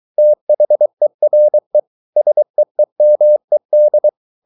Morse code, heard as a keyed tone, about 23 wpm, 600 Hz. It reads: THERE SEEMED